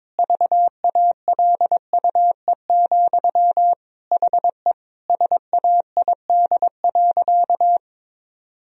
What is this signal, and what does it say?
Morse code, heard as a keyed tone, about 22 wpm, 700 Hz. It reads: VALUE, HE SAID.
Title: Sentences from Top 1000 Words - Review Part 3 - 22wpm